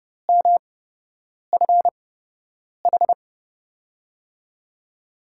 Morse code, heard as a keyed tone, about 30 wpm, spaced as for 8 wpm, 700 Hz. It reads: MFH
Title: USA Call Signs - 6 Characters - 8x Character Spacing - 30wpm